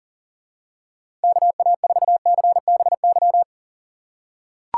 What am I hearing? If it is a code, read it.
Morse code, heard as a keyed tone, about 37 wpm, 700 Hz. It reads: KA4CBY
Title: USA Call Signs - 6 Characters - 4x Character Spacing - 40wpm